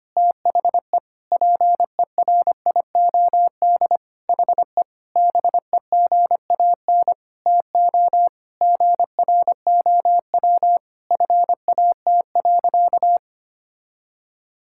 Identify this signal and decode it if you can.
Morse code, heard as a keyed tone, about 25 wpm, 700 Hz. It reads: THE PERIOD HE BEGAN TO GROW FAT.
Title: Sentences from Top 1000 Words - Review Part 5 - 25wpm